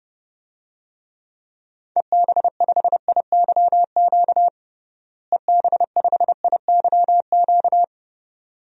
Morse code, heard as a keyed tone, about 31 wpm, 700 Hz. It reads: EB5IYQ EB5IYQ
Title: DXCC-100 Call Signs - 6 Characters - Repeated 2x - 30wpm